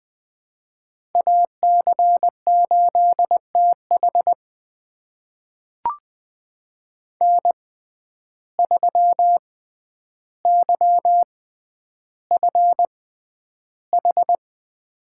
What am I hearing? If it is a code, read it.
Morse code, heard as a keyed tone, about 20 wpm, spaced as for 10 wpm, 700 Hz. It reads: AC8TH N3YFH
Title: USA Call Signs - 6x Character Spacing - 20wpm